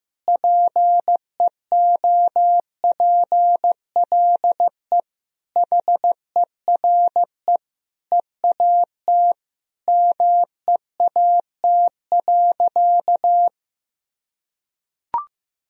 Morse code, heard as a keyed tone, about 15 wpm, 700 Hz. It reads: PEOPLE HERE EAT MEAT.
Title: Sentences from Top 1000 Words - Review Part 3 - 15wpm